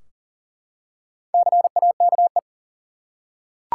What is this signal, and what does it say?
Morse code, heard as a keyed tone, about 39 wpm, 700 Hz. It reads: CAKE
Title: Four Letter Words - 40wpm